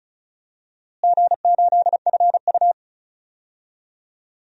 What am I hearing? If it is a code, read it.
Morse code, heard as a keyed tone, about 34 wpm, 700 Hz. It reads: G8FU
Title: DXCC-100 Call Signs - 4 Characters - Repeated 2x - 35wpm